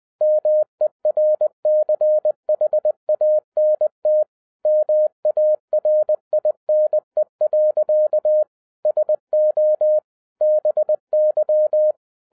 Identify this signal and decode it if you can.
Morse code, heard as a keyed tone, about 20 wpm, 610 Hz. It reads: MERCHANT MARINE. SO BY
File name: 2011-07-mp3 / n08 A LOT HAS CHANGED 20wpm